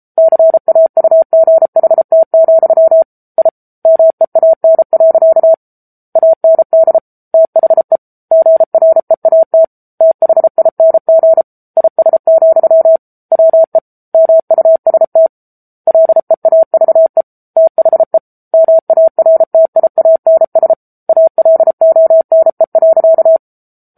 Morse code, heard as a keyed tone, about 34 wpm, 650 Hz. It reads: CAUGHT, I MEAN. AND THE GREAT THING IS, WE MUST LEAVE THE MARTIANS ALONE.
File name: Book Two / 07-Chapter 7 - 33 WPM